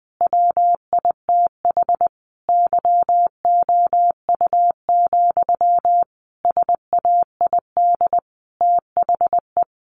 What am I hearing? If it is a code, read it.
Morse code, heard as a keyed tone, about 20 wpm, 700 Hz. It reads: WITH YOU, SAID THE
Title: Sentences from Top 600 Words 20wpm